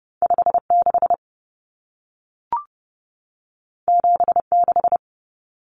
Morse code, heard as a keyed tone, about 30 wpm, 700 Hz. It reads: 56 76
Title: Common Cut Numbers - 2-Digit 30wpm